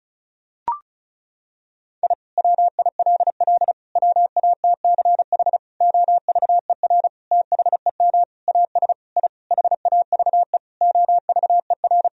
Morse code, heard as a keyed tone, about 34 wpm, 700 Hz. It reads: I WILL WATCH OVER THEM AS I HAVE OVER
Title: Sentences from Top 1000 Words - Review Part 1 - 35wpm